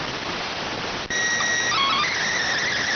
SSTV (slow-scan television), PD 290.